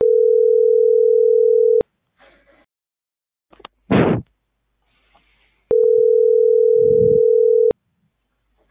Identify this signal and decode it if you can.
Ringback tone (North America), 440 + 480 Hz.